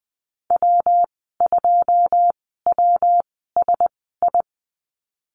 Morse code, heard as a keyed tone, about 20 wpm, spaced as for 15 wpm, 700 Hz. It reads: W2WSI
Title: 173 Lesson 28g - Number 9 - Call Signs 20wpm